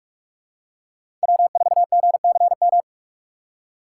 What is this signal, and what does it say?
Morse code, heard as a keyed tone, about 45 wpm, 700 Hz. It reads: W4GCM
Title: USA Call Signs - 8x Character Spacing - 45wpm